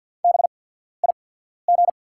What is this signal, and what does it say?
Morse code, heard as a keyed tone, about 48 wpm, spaced as for 14 wpm, 690 Hz. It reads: BIG
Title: 8x Character Spacing - Top 500 Words - 50wpm